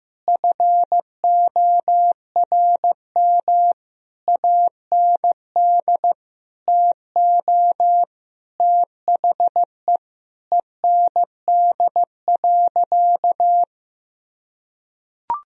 Morse code, heard as a keyed tone, about 15 wpm, 700 Hz. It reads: FORM AND TO THE END.
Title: Sentences from Top 1000 Words - Review Part 1 - 15wpm